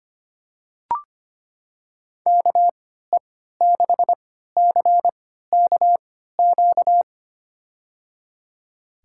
Morse code, heard as a keyed tone, about 25 wpm, spaced as for 14 wpm, 700 Hz. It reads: KE6CKQ
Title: USA Call Signs - 3x Character Spacing - 25wpm